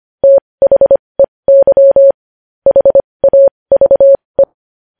Morse code, heard as a keyed tone, about 25 wpm, spaced as for 20 wpm, 550 Hz. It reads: THEY HAVE
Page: News headlines in Morse code at 20 wpm